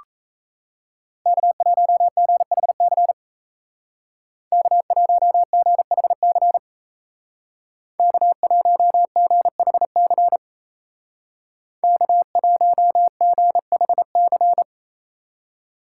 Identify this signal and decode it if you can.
Morse code, heard as a keyed tone, about 34 wpm, 700 Hz. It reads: K1GHC K1GHC K1GHC K1GHC
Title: Speed Racing - USA 1x3 Call Signs 28wpm